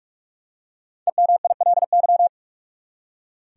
Morse code, heard as a keyed tone, about 46 wpm, 700 Hz. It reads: EMILY
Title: USA Common Names - Max 5 Letters - 45wpm